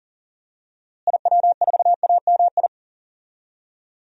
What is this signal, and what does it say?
Morse code, heard as a keyed tone, about 40 wpm, 700 Hz. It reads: IW4AMI
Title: DXCC-100 Call Signs - 6 Characters - Repeated 3x - 40wpm